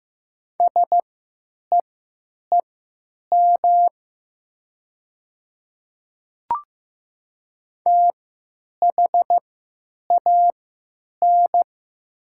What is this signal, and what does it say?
Morse code, heard as a keyed tone, about 15 wpm, spaced as for 9 wpm, 700 Hz. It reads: SEEM THAN
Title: 3x Character Spacing - Top 500 Words - 15wpm